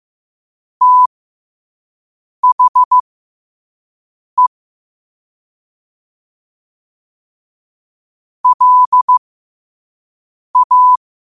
Morse code, heard as a keyed tone, about 15 wpm, spaced as for 5 wpm, 1000 Hz. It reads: THE LA
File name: qotdfiles / 5wpm